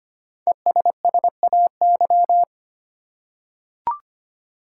Morse code, heard as a keyed tone, about 25 wpm, 700 Hz. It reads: ESSAY
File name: 5 Letter Words - First 3 Letters Given - 25wpm